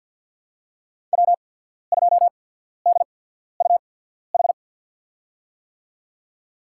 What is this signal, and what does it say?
Morse code, heard as a keyed tone, about 54 wpm, spaced as for 14 wpm, 700 Hz. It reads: W2DUH
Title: USA Call Signs - 5 Characters - 8x Character Spacing - 50wpm